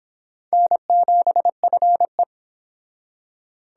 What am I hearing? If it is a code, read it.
Morse code, heard as a keyed tone, about 25 wpm, 700 Hz. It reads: N7FE